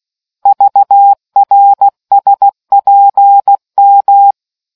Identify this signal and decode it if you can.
Morse code, heard as a keyed tone, about 16 wpm, 800 Hz. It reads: VRSPM